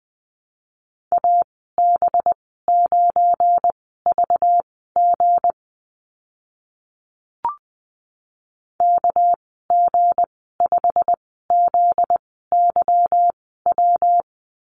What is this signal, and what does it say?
Morse code, heard as a keyed tone, about 20 wpm, spaced as for 14 wpm, 700 Hz. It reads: AB9VG KG5ZYW
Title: 245 Lesson 39g - Letter z - Call Signs 20wpm